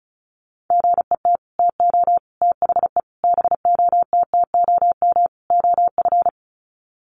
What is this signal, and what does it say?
Morse code, heard as a keyed tone, about 34 wpm, 700 Hz. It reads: GET TO THE BOTTOM OF
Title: Sets of 5 Words Encore 35wpm